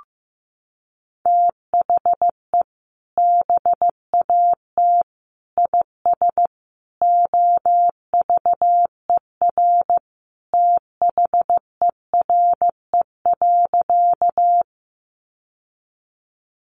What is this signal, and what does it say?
Morse code, heard as a keyed tone, about 15 wpm, 700 Hz. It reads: THE BAT IS OVER THERE.
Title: Sentences from Top 1500 Words 15wpm